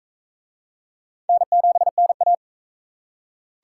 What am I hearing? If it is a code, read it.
Morse code, heard as a keyed tone, about 39 wpm, 700 Hz. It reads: N7NA